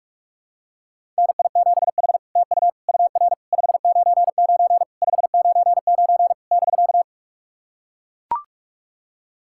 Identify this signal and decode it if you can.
Morse code, heard as a keyed tone, about 44 wpm, 660 Hz. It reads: NI7H TU UR 599 599 <BK>
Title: SOTA QSO Practice 45wpm